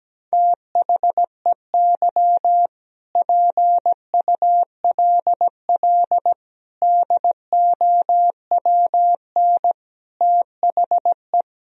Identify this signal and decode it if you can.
Morse code, heard as a keyed tone, about 17 wpm, 700 Hz. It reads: THEY PULL DOWN THE
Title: Sentences from Top 1000 Words - Review Part 2 - 17wpm